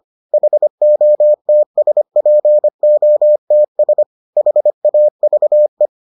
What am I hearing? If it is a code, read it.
Morse code, heard as a keyed tone, about 25 wpm, 600 Hz. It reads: HOTSPOTS HAVE